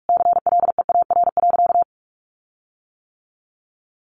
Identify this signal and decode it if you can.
Morse code, heard as a keyed tone, about 44 wpm, 700 Hz. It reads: CLEAR.
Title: Sentences from Top 1300 Words 45wpm